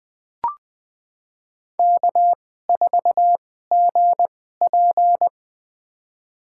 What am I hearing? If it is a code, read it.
Morse code, heard as a keyed tone, about 20 wpm, spaced as for 14 wpm, 700 Hz. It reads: K4GP
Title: USA Call Signs - 4 Characters - 2x Character Spacing - 20wpm